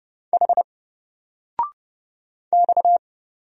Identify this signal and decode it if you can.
Morse code, heard as a keyed tone, about 29 wpm, 700 Hz. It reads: H X
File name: Single-Letter-Number-30wpm